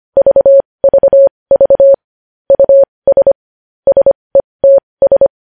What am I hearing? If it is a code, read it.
Morse code, heard as a keyed tone, about 25 wpm, spaced as for 20 wpm, 550 Hz. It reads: VVV US SETS
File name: podcast / News headlines-1x-25x20